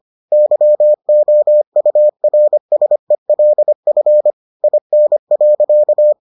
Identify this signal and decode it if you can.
Morse code, heard as a keyed tone, about 25 wpm, 600 Hz. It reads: YOURSELF IN.